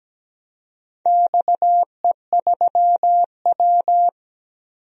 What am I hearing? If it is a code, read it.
Morse code, heard as a keyed tone, about 17 wpm, 700 Hz. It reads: XE3W
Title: DXCC-100 Call Signs - 4 Characters - 17wpm